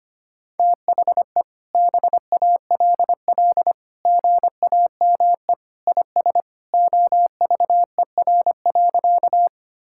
Morse code, heard as a keyed tone, about 25 wpm, 700 Hz. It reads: THE BALL GAME IS OVER.
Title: Sentences from Top 1000 Words - Review Part 6 - 25wpm